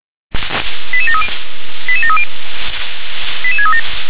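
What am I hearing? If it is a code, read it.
Five-tone selective call (ZVEI), heard as quick runs of five tones, tones of about 70 ms. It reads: 88730 88730 88637
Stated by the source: ZVEI five-tone selective call received on SDR.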